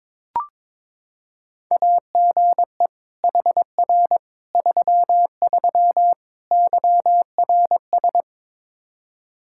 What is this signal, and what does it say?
Morse code, heard as a keyed tone, about 22 wpm, 700 Hz. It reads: AGE HR 33 YRS
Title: Practice QSO - Exchange 3 - More Info - 22wpm